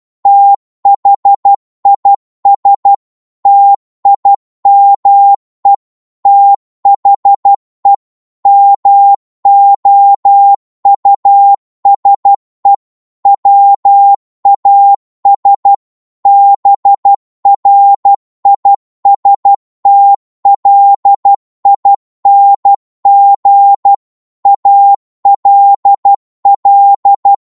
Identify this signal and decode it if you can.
Morse code, heard as a keyed tone, about 12 wpm, 800 Hz. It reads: THIS TIME THE MOUSE WAS BRISTLING ALL
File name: Morse / psg-12wpm